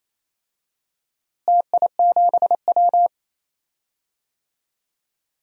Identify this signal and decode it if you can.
Morse code, heard as a keyed tone, about 28 wpm, 700 Hz. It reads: TI7W
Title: DXCC-100 Call Signs - 4 Characters - Repeated 3x - 28wpm